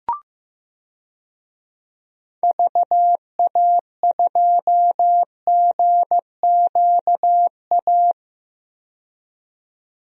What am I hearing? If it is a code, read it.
Morse code, heard as a keyed tone, about 15 wpm, 700 Hz. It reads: VA2GQA
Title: DXCC-100 Call Signs - 6 Characters - 15wpm